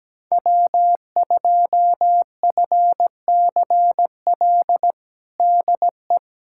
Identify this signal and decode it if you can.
Morse code, heard as a keyed tone, about 17 wpm, 700 Hz. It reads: W2FCL DE
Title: USA Callsign DE Callsign 17wpm